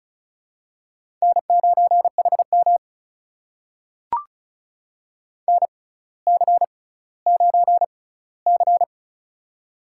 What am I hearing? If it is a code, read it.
Morse code, heard as a keyed tone, about 36 wpm, spaced as for 18 wpm, 700 Hz. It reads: N9HM NC9C
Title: USA Call Signs - 6x Character Spacing - 35wpm